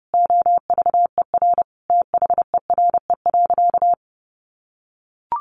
Morse code, heard as a keyed tone, about 28 wpm, 700 Hz. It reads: OVER THERE.
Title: Sentences from Top 1100 Words 30wpm